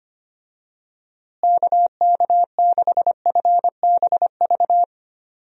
Morse code, heard as a keyed tone, about 25 wpm, 700 Hz. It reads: KK6FBV
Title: USA Call Signs - 6 Characters - Repeated 3x - 25wpm